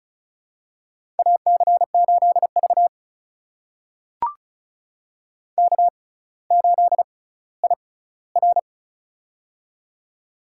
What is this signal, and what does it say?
Morse code, heard as a keyed tone, about 36 wpm, spaced as for 18 wpm, 700 Hz. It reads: AC8V K8IR
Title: USA Call Signs - 4 Characters - 6x Character Spacing - 35wpm